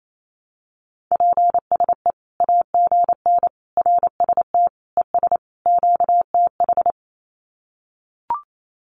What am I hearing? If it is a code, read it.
Morse code, heard as a keyed tone, about 28 wpm, 700 Hz. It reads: PSE AGN RST ES QTH
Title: Common QSO Phrases 28wpm